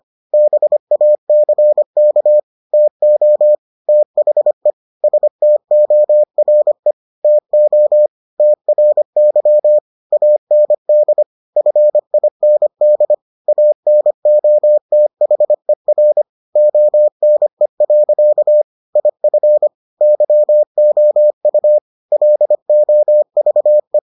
Morse code, heard as a keyed tone, about 25 wpm, 600 Hz. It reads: BACK TO THE STORE TO TRY AND FIND ANOTHER ONE. IF YOU LOVE